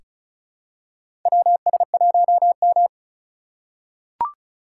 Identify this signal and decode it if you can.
Morse code, heard as a keyed tone, about 36 wpm, 700 Hz. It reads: WS1M
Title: USA Call Signs - 4 Characters - 2x Character Spacing - 35wpm